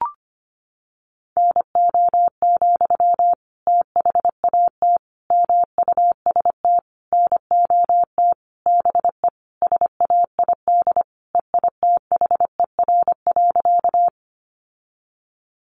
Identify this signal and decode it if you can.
Morse code, heard as a keyed tone, about 25 wpm, 700 Hz. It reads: NO, THAT MUST NOT BE SAID EITHER.
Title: Sentences from Top 600 Words 25wpm